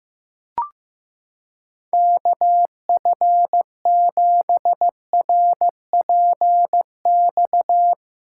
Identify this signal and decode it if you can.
Morse code, heard as a keyed tone, about 16 wpm, 700 Hz. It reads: KF7RPX
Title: USA Callsign DE Callsign - 6 Characters - 15wpm